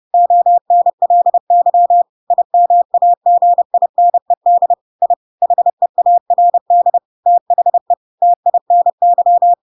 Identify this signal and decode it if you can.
Morse code, heard as a keyed tone, about 30 wpm, 700 Hz. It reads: ONLY IMAGINED I HEARD THE TINY
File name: Pinocchio / Pinocchio ch1 30 30 ik1hge